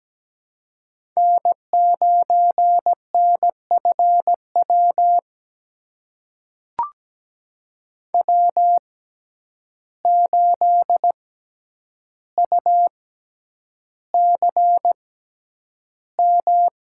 Morse code, heard as a keyed tone, about 17 wpm, spaced as for 9 wpm, 700 Hz. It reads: N9NFW W8UCM
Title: USA Call Signs - 6x Character Spacing - 17wpm